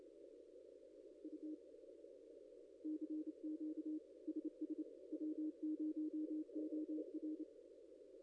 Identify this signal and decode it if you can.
Morse code, heard as a keyed tone, about 30 wpm, 310 Hz. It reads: U CQ SS W0OR
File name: contests / ss-cw-k5tr-op-n5rz -0900